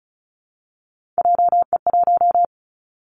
Morse code, heard as a keyed tone, about 36 wpm, 700 Hz. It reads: JE1
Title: Callsign Prefixes for Top 25 DXCC - 35wpm